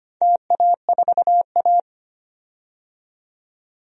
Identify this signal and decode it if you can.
Morse code, heard as a keyed tone, about 25 wpm, 700 Hz. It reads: TA4A
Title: DXCC-100 Call Signs - 4 Characters - 25wpm